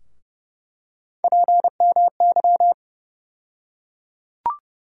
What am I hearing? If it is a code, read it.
Morse code, heard as a keyed tone, about 30 wpm, 700 Hz. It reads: PMY
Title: USA Call Sign Postfixes - 3 Letters 30wpm